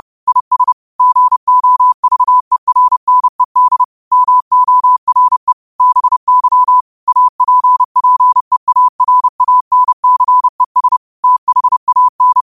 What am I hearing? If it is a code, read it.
Morse code, heard as a keyed tone, about 30 wpm, 1000 Hz. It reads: IS GOVERNED MORE BY APPEARANCES THAN